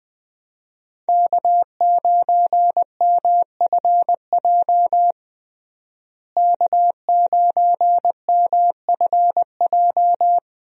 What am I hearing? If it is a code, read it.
Morse code, heard as a keyed tone, about 21 wpm, 700 Hz. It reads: K9MFJ K9MFJ
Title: USA Call Signs - 5 Characters - Repeated 3x - 20wpm